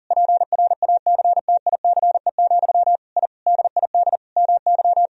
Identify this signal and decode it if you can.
Morse code, heard as a keyed tone, about 40 wpm, 700 Hz. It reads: PRACTICE, I DID MY
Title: Sentences from Top 1000 Words - Review Part 4 - 40wpm